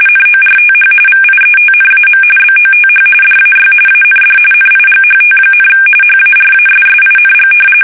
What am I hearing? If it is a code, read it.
RTTY, about 45.45 baud. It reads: NORTH VIETNAM BOTH DAY AND NIGHT, HARASSING
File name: fsk850 / battleshipnj